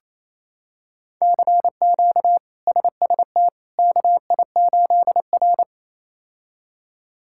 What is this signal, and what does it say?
Morse code, heard as a keyed tone, about 28 wpm, 700 Hz. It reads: CQ SST KI8R
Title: K1USN SST QSO Practice - 28wpm